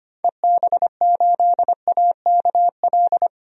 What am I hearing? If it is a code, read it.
Morse code, heard as a keyed tone, about 25 wpm, 700 Hz. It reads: EB8AKL
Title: DXCC-100 Call Signs - 6 Characters - 25wpm